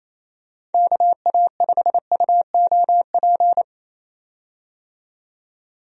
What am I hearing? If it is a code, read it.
Morse code, heard as a keyed tone, about 28 wpm, 700 Hz. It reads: KA5UOP
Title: USA Call Signs - 6 Characters - 28wpm